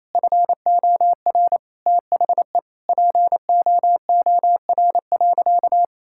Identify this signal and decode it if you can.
Morse code, heard as a keyed tone, about 27 wpm, 700 Hz. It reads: FOR THE POOR.
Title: Sentences from Top 1600 Words 28wpm